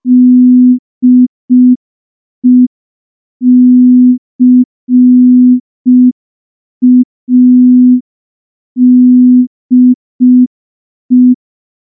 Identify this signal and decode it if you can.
Morse code, heard as a keyed tone, about 5 wpm, 250 Hz. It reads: DECADE
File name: morse / morse-training-05WPM